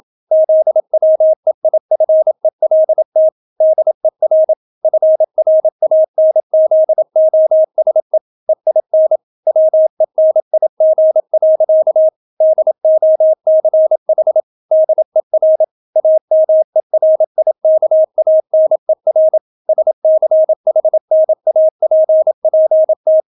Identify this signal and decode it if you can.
Morse code, heard as a keyed tone, about 27 wpm, 630 Hz. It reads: ZWEIFELT DER FRANZOSE EIN WENIG. DOCH DER AMERIKANER SCHNAPPT